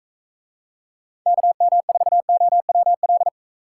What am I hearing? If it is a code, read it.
Morse code, heard as a keyed tone, about 44 wpm, 700 Hz. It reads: KM4OWL